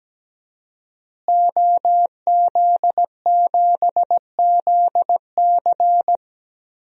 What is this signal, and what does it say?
Morse code, heard as a keyed tone, about 17 wpm, 700 Hz. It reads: OZ7ZC
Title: DXCC-100 Call Signs - 5 Characters - Repeated 2x - 17wpm